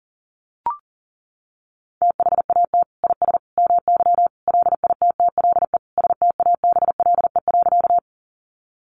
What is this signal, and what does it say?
Morse code, heard as a keyed tone, about 40 wpm, 700 Hz. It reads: THAT IS MY LITTLE STABLE.
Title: Sentences from Top 1500 Words 40wpm